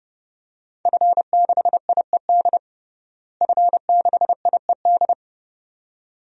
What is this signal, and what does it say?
Morse code, heard as a keyed tone, about 30 wpm, 700 Hz. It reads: F6IED F6IED
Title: DXCC-100 Call Signs - 5 Characters - Repeated 2x - 30wpm